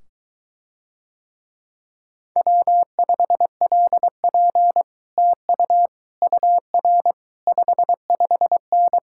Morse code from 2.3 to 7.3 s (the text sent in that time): W5LP TU UR